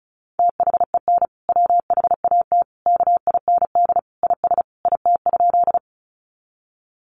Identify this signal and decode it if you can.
Morse code, heard as a keyed tone, about 34 wpm, 700 Hz. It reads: THEN WHAT KIND IS IT?
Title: Sentences from Top 200 Words 35wpm